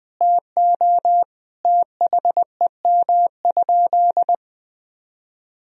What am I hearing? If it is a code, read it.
Morse code, heard as a keyed tone, about 20 wpm, 700 Hz. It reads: TO THEM?
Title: Sentences from Top 1000 Words - Review Part 2 - 20wpm